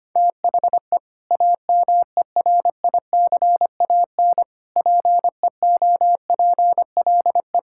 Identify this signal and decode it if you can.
Morse code, heard as a keyed tone, about 25 wpm, 700 Hz. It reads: THE AMERICAN PEOPLE
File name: practice / 25wpm / Sentences with 3-Word Phrases - 25wpm